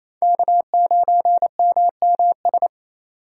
Morse code, heard as a keyed tone, about 28 wpm, 700 Hz. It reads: K9MMS
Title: K1USN SST QSO Practice - 28wpm